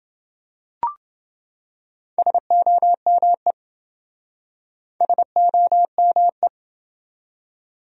Morse code, heard as a keyed tone, about 26 wpm, 700 Hz. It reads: SOME SOME